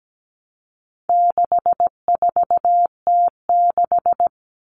Morse code, heard as a keyed tone, about 17 wpm, 700 Hz. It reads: 64T6
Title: Common Cut Numbers - 4-Digit 17wpm